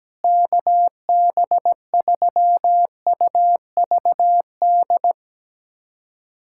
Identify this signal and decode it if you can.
Morse code, heard as a keyed tone, about 17 wpm, 700 Hz. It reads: KB3UVD
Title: USA Callsign DE Callsign 17wpm